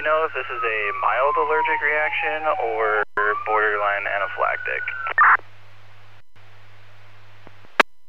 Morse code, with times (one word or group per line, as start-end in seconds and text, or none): none